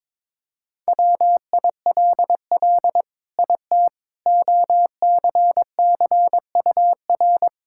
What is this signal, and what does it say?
Morse code, heard as a keyed tone, about 22 wpm, 700 Hz. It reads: WILL IT OCCUR
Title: Sentences from Top 1000 Words - Review Part 6 - 22wpm